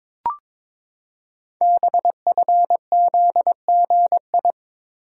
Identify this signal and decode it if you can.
Morse code, heard as a keyed tone, about 22 wpm, 700 Hz. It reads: BFZGI
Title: Code Groups - 5 Letters - 22wpm